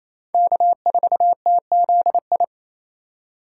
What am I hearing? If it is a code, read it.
Morse code, heard as a keyed tone, about 35 wpm, 700 Hz. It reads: K4TZI
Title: Speed Racing - USA 1x3 Call Signs 28wpm